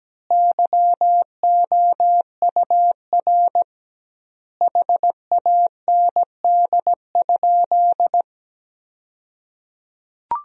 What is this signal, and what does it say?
Morse code, heard as a keyed tone, about 17 wpm, 700 Hz. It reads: YOUR HAND?
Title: 2x Word-Spacing - Sentences from Top 200 Words - 17wpm